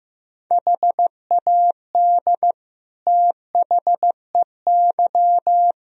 Morse code, heard as a keyed tone, about 15 wpm, 700 Hz. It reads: HAD THEY
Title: Sentences from Top 1000 Words - Review Part 2 - 15wpm